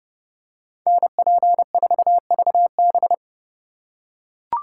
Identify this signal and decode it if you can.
Morse code, heard as a keyed tone, about 28 wpm, 700 Hz. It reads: NP4VB